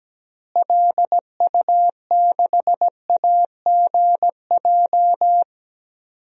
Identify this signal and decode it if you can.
Morse code, heard as a keyed tone, about 17 wpm, 700 Hz. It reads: LU6AGJ
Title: DXCC-100 Call Signs - 6 Characters - Repeated 3x - 17wpm